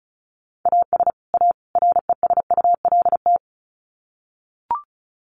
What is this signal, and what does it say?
Morse code, heard as a keyed tone, about 34 wpm, 700 Hz. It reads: AS A RESULT
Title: Sets of 3 Words 35wpm